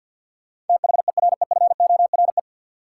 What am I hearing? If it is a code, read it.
Morse code, heard as a keyed tone, about 53 wpm, 700 Hz. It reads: THEREFORE
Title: Sentences from Top 800 Words - New Word Drill - 50wpm